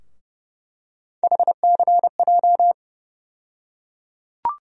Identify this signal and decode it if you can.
Morse code, heard as a keyed tone, about 30 wpm, 700 Hz. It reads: HCJ
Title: USA Call Sign Postfixes - 3 Letters 30wpm